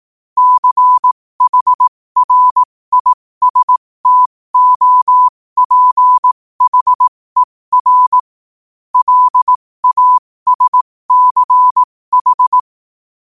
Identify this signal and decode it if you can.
Morse code, heard as a keyed tone, about 18 wpm, spaced as for 15 wpm, 1000 Hz. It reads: CHRISTOPHER LASCH
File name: qotdfiles / 15wpm